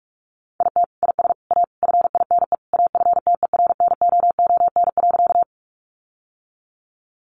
Morse code, heard as a keyed tone, about 45 wpm, 700 Hz. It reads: IT IS A FINE AFTERNOON.
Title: Sentences from Top 1500 Words 45wpm